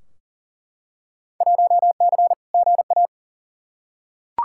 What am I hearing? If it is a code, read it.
Morse code, heard as a keyed tone, about 40 wpm, 700 Hz. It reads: 1C GA
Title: Field Day Exchanges Encore - 40wpm